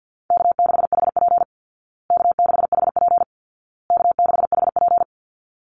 Morse code, heard as a keyed tone, about 50 wpm, 700 Hz. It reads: K6HP K6HP K6HP
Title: Call Signs Repeated 2x - 50wpm